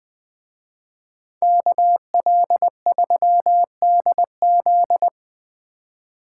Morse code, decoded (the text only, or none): KL3DZ